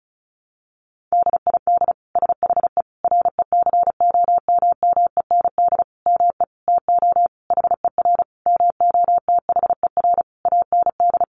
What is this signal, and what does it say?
Morse code, heard as a keyed tone, about 34 wpm, 700 Hz. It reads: DID SHE RECOMMEND ME TO HER MOTHER AND
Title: Sentences from Top 1200 Words 35wpm